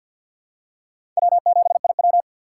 Morse code, heard as a keyed tone, about 51 wpm, 700 Hz. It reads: W7IW
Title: USA Call Signs - 4 Characters - 2x Character Spacing - 50wpm